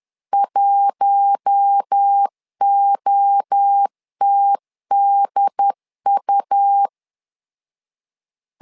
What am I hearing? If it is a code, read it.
Morse code, heard as a keyed tone, about 10 wpm, 800 Hz. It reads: 1OTDU
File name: LEZIONI / 4 Q1S 10wpm 2133